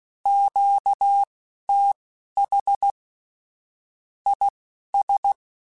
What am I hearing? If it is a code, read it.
Morse code, heard as a keyed tone, about 16 wpm, spaced as for 11 wpm, 800 Hz. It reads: QTH IS
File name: QSOs / QSOs-mp3s / kc7haa@10